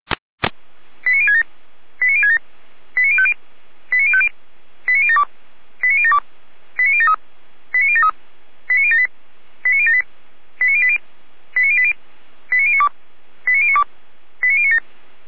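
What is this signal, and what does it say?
Five-tone selective call (ZVEI), heard as quick runs of five tones, tones of about 70 ms. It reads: 89067 89067 89050 89050 89072 89072 89073 89073 89078 89078 89080 89080 89092 89092 89097
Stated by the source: ZVEI five-tone selective call 89012, received by SDR.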